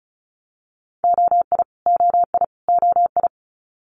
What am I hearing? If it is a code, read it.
Morse code, heard as a keyed tone, about 36 wpm, 700 Hz. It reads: OI OI OI
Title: Multi-letter Phonemes Repeated 3x 35wpm